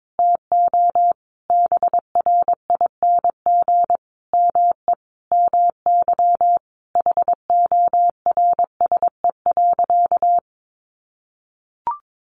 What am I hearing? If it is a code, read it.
Morse code, heard as a keyed tone, about 22 wpm, 700 Hz. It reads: TO BRING ME MY HORSE.
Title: Sentences from Top 500 Words 22wpm